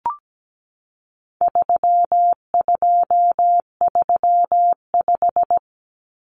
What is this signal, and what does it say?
Morse code, heard as a keyed tone, about 17 wpm, 700 Hz. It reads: 3235
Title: Numbers - 4-Digit 17wpm